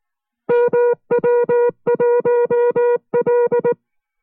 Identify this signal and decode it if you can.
Morse code, heard as a keyed tone, about 19 wpm, 480 Hz. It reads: MW1L